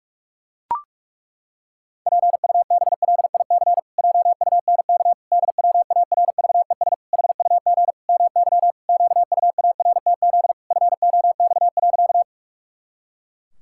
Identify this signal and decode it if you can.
Morse code, heard as a keyed tone, about 41 wpm, 700 Hz. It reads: PUBLIC JUNK DWARVES HUG MY QUARTZ FOX.
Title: Pangram Challenge v2 - 45wpm